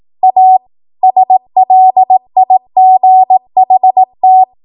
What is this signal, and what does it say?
Morse code, heard as a keyed tone, about 18 wpm, 750 Hz. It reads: A SLIGHT
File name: Morse / 18WPM